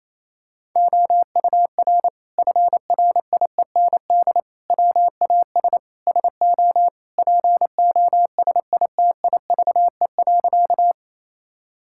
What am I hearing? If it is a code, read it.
Morse code, heard as a keyed tone, about 28 wpm, 700 Hz. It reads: OUR FRIEND WAS SO POSITIVE.
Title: Sentences from Top 1000 Words - Review Part 5 - 28wpm